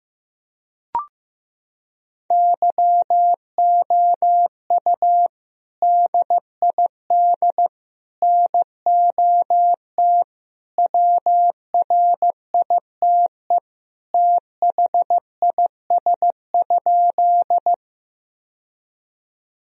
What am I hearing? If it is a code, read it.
Morse code, heard as a keyed tone, about 16 wpm, 700 Hz. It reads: YOU DID NOT WRITE THIS?
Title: Sentences from Top 1000 Words - Review Part 1 - 15wpm